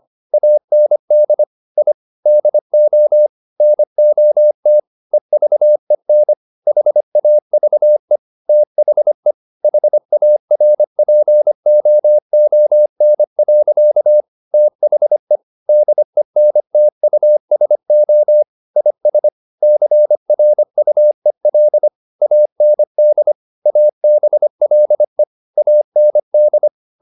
Morse code, heard as a keyed tone, about 25 wpm, 600 Hz. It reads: AND I DO NOT EVEN HAVE THE HARPOON. THE DENTUSO IS CRUEL AND ABLE AND